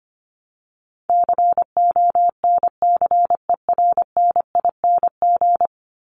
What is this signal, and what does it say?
Morse code, heard as a keyed tone, about 25 wpm, 700 Hz. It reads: CONCERNING
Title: ING Suffix - Top 100 Words with Base Words 25wpm